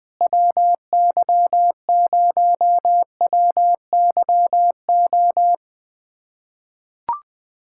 Morse code, heard as a keyed tone, about 20 wpm, 700 Hz. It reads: WY0WYO
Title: SOTA QSO Practice 20wpm